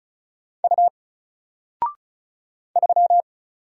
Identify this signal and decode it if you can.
Morse code, heard as a keyed tone, about 33 wpm, 700 Hz. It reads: U 3
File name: Single-Letter-Number-35wpm